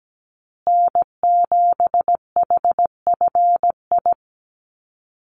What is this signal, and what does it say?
Morse code, heard as a keyed tone, about 17 wpm, 700 Hz. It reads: N7HFI
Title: USA Call Signs Repeated 2x 17wpm